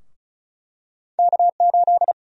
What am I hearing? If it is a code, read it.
Morse code, heard as a keyed tone, about 36 wpm, 700 Hz. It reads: K8
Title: USA Call Sign Prefixes - Repeated 2x - 35wpm